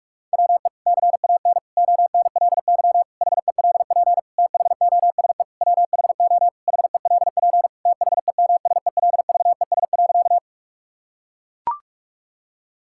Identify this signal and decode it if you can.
Morse code, heard as a keyed tone, about 41 wpm, 700 Hz. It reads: WE CAN ONLY HELP THOSE WHO HELP THEMSELVES.
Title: Sentences from Top 1000 Words - Review Part 5 - 45wpm